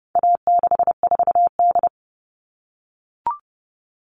Morse code, heard as a keyed tone, about 30 wpm, 700 Hz. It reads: A64D
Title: All Cut Numbers - 4-Digit 30wpm